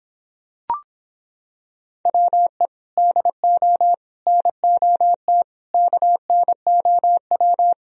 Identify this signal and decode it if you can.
Morse code, heard as a keyed tone, about 26 wpm, 700 Hz. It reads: WE DO NOT KNOW